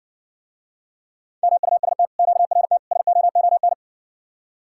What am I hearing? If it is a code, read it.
Morse code, heard as a keyed tone, about 61 wpm, 700 Hz. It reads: MUST QRT SOON